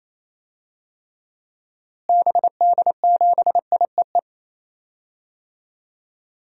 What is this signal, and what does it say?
Morse code, heard as a keyed tone, about 27 wpm, 700 Hz. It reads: BD7IEE
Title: DXCC-100 Call Signs - 6 Characters - Repeated 2x - 28wpm